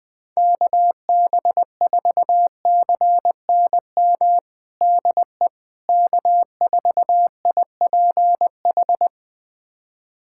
Morse code, heard as a keyed tone, about 20 wpm, 700 Hz. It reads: KB4CNM DE K4IPH
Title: USA Callsign DE Callsign 20wpm